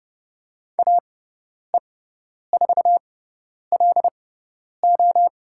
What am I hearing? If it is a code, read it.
Morse code, heard as a keyed tone, about 30 wpm, spaced as for 10 wpm, 700 Hz. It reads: AE4LO
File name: Lesson 18g - Letter m - Call Signs 30wpm